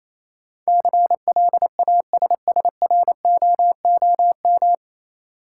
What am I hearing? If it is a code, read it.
Morse code, heard as a keyed tone, about 27 wpm, 700 Hz. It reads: CLASSROOM
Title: Sentences from Top 1500 Words - New Word Drill Alt - 28wpm